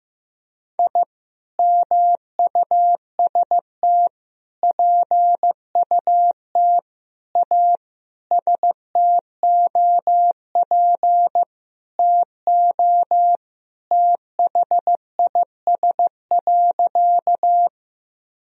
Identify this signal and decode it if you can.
Morse code, heard as a keyed tone, about 16 wpm, 700 Hz. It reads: I MUST PUT A STOP TO THIS.
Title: Sentences from Top 1000 Words - Review Part 1 - 15wpm